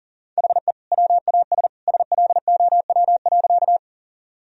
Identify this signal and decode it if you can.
Morse code, heard as a keyed tone, about 40 wpm, 700 Hz. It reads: HE WAS SLOW.
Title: Sentences from Top 1000 Words - Review Part 2 - 40wpm